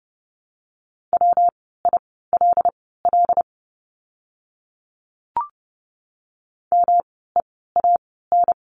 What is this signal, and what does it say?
Morse code, heard as a keyed tone, about 29 wpm, spaced as for 17 wpm, 700 Hz. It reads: WILL MEAN
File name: course / 30:10wpm / 237 Lesson 38f - Letter q - Words 30wpm